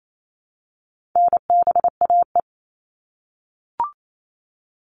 Morse code, heard as a keyed tone, about 28 wpm, 700 Hz. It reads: NBAE
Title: All Cut Numbers - 4-Digit 28wpm